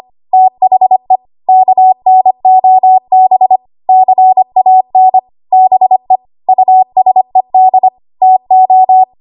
Morse code, heard as a keyed tone, about 25 wpm, 750 Hz. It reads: THE KNOB CAN BE USED TO
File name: Archive / 25 WPM / 25WPM